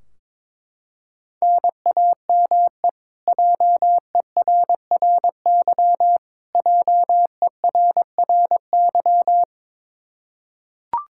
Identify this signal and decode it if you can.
Morse code, heard as a keyed tone, about 21 wpm, 700 Hz. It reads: NAME JERRY JERRY
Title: Practice QSO - Exchange 1 - Beginning - 22wpm